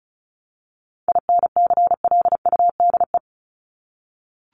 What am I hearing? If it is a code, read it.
Morse code, heard as a keyed tone, about 34 wpm, 700 Hz. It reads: INCLUDE
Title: ED Suffix - Top 100 Words with Base Words 35wpm